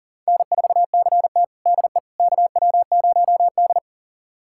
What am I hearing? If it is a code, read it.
Morse code, heard as a keyed tone, about 37 wpm, 690 Hz. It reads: N4CT DE KW0D
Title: USA Callsign DE Callsign - 4 Characters - 40wpm